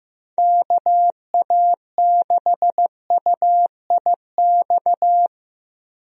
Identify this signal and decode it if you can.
Morse code, heard as a keyed tone, about 15 wpm, 700 Hz. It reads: KA6UIX
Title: USA Call Signs - 6 Characters - Repeated 2x - 15wpm